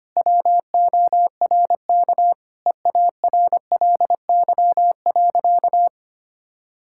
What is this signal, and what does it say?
Morse code, heard as a keyed tone, about 25 wpm, 700 Hz. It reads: WORK EARLY.
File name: practice / 25wpm / Sentences with 2-Word Phrases - 25wpm